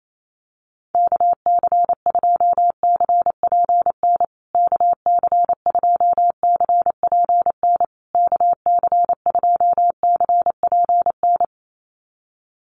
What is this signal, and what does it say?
Morse code, heard as a keyed tone, about 28 wpm, 700 Hz. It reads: KC2CPN KC2CPN KC2CPN
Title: USA Call Signs Repeated 3x 28wpm